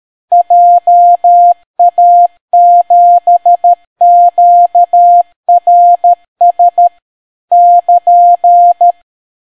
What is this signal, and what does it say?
Morse code, heard as a keyed tone, about 13 wpm, 700 Hz. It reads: JA7QRS <KN>